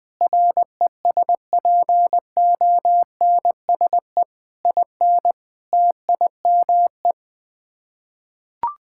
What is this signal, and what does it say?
Morse code, heard as a keyed tone, about 20 wpm, 700 Hz. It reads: RESPONSE IN TIME
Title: Sets of 8 Words - 20wpm